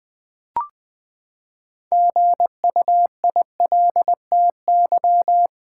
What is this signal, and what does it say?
Morse code, heard as a keyed tone, about 20 wpm, 700 Hz. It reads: GUILTY
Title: Six Letter Words - 20wpm